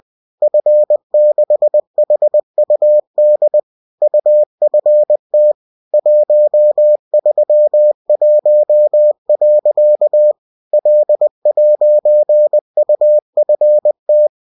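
Morse code, heard as a keyed tone, about 20 wpm, 600 Hz. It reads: F6HUD UFT 131. L'UFT
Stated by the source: Bulletin sent in CW.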